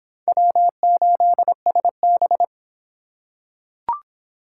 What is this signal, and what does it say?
Morse code, heard as a keyed tone, about 25 wpm, 700 Hz. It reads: W8SB